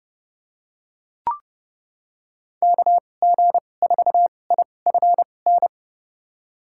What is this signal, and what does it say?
Morse code, heard as a keyed tone, about 30 wpm, spaced as for 22 wpm, 700 Hz. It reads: KG4IFN
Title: USA Call Signs - 6 Characters - 2x Character Spacing - 30wpm